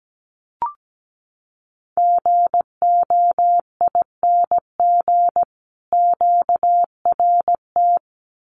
Morse code, heard as a keyed tone, about 17 wpm, 700 Hz. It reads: GOING QRT
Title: Common QSO Phrases 17wpm